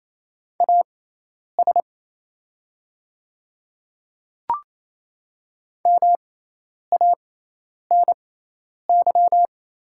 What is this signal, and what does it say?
Morse code, heard as a keyed tone, about 27 wpm, spaced as for 10 wpm, 700 Hz. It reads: AS MANY
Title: 6x Character Spacing - Top 100 Words - 28wpm